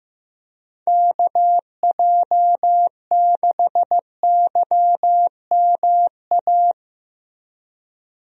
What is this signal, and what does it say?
Morse code, heard as a keyed tone, about 15 wpm, 700 Hz. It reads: KJ6YMA
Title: USA Call Signs - 6 Characters - Repeated 3x - 15wpm